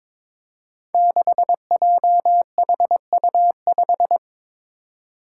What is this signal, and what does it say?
Morse code, heard as a keyed tone, about 22 wpm, 700 Hz. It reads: 6JHU5
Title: Five Challenging Letter-Number Combinations 22wpm